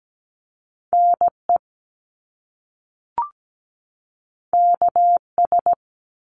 Morse code, heard as a keyed tone, about 17 wpm, 700 Hz. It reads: NE KS
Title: US States 17wpm